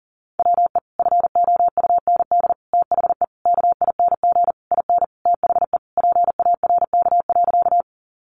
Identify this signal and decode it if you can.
Morse code, heard as a keyed tone, about 40 wpm, 700 Hz. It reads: WE FOUND THE KING IN THE PARK.
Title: Sentences from Top 1300 Words 40wpm